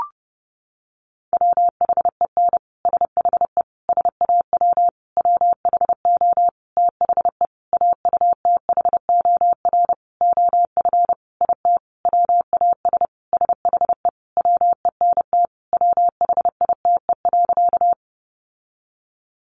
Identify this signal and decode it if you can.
Morse code, heard as a keyed tone, about 30 wpm, 700 Hz. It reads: WHEN SHE SAW WHO THE AUTHOR OF IT WAS SHE WENT WHITE.
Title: Sentences from Top 1200 Words 30wpm